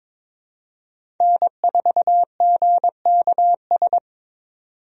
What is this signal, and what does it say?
Morse code, heard as a keyed tone, about 21 wpm, 700 Hz. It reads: N4GKS
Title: USA Call Signs - 5 Characters - 22wpm